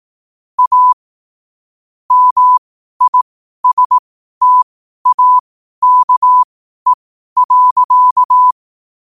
Morse code, heard as a keyed tone, about 18 wpm, spaced as for 12 wpm, 1000 Hz. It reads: A MISTAKE.
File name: qotdfiles / 12wpm